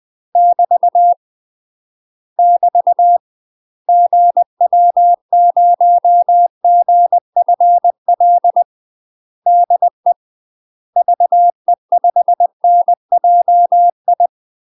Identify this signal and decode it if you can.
Morse code, heard as a keyed tone, about 20 wpm, 700 Hz. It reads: <BT> <BT> GW0GFL DE VE5NJI
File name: UK MASTER QSOs TRAINING / 20WPM / UK MASTER QSOs 0073